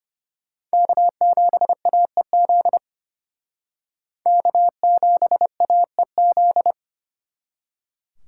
Morse code, heard as a keyed tone, about 32 wpm, 700 Hz. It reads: K7AEZ K7AEZ